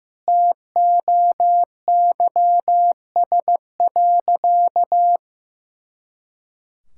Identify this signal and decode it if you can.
Morse code, heard as a keyed tone, about 15 wpm, 700 Hz. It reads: TOYS.
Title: Pangram Challenge v2 - 15wpm